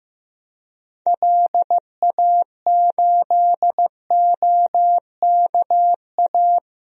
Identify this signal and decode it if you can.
Morse code, heard as a keyed tone, about 15 wpm, 700 Hz. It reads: LA8OKA
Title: DXCC-100 Call Signs - 6 Characters - 15wpm